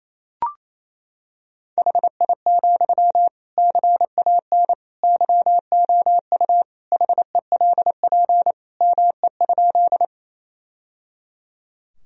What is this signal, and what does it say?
Morse code, heard as a keyed tone, about 28 wpm, 700 Hz. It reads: HI, CAN YOU HELP ME?
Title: Sentences from Top 1000 Words - Review Part 3 - 28wpm